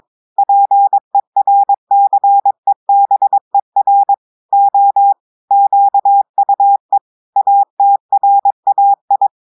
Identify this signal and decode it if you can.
Morse code, heard as a keyed tone, about 22 wpm, 800 Hz. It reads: PERCEBER O QUE ATRAI
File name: audios / portugais / portugais texte en clair 2 vitesse 22 mots minutes